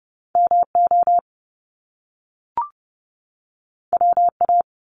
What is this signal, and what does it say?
Morse code, heard as a keyed tone, about 30 wpm, 700 Hz. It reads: MO WA
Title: Top Two Letter Combinations in Words 30wpm